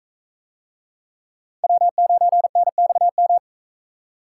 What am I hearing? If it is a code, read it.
Morse code, heard as a keyed tone, about 44 wpm, 700 Hz. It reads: W9NXM